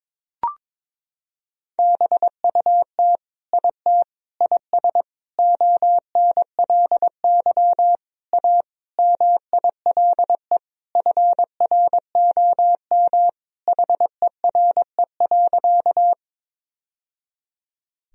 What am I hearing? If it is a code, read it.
Morse code, heard as a keyed tone, about 22 wpm, 700 Hz. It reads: BUT IT IS ONLY A MILE FROM HERE.
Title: Sentences from Top 1000 Words - Review Part 1 - 22wpm